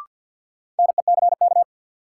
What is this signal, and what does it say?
Morse code, heard as a keyed tone, about 53 wpm, 700 Hz. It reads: NECK
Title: Sets of 5 Words Encore - New Word Drill - 50wpm